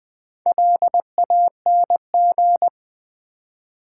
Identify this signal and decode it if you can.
Morse code, heard as a keyed tone, about 20 wpm, 700 Hz. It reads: LANG
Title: Build A Word - 4 Letters At A Time - 8-Letter Words - 20wpm